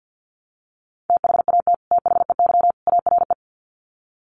Morse code, heard as a keyed tone, about 50 wpm, 700 Hz. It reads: THAT THEY ARE
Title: Sets of 3 Words 50wpm